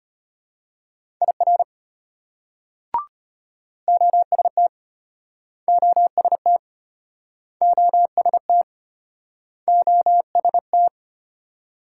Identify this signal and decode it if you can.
Morse code, heard as a keyed tone, about 33 wpm, 700 Hz. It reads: IR OST OST OST OST